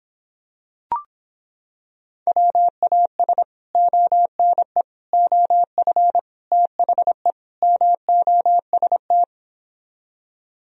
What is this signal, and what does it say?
Morse code, heard as a keyed tone, about 26 wpm, 700 Hz. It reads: WAS ONE OF THE MOST